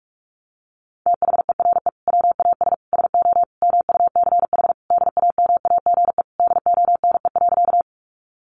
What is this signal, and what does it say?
Morse code, heard as a keyed tone, about 44 wpm, 700 Hz. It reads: THERE WAS SO MUCH DAMAGE DONE.
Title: Sentences from Top 1100 Words 45wpm